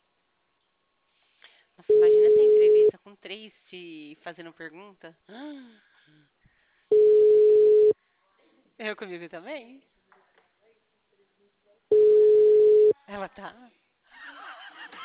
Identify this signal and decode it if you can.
Ringback tone (Europe, 425 Hz), with speech.